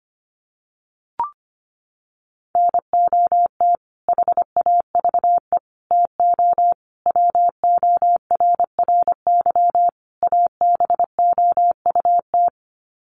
Morse code, heard as a keyed tone, about 25 wpm, 700 Hz. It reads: NOT HAVE TO WORRY ABOUT
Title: Sets of 5 Words 25wpm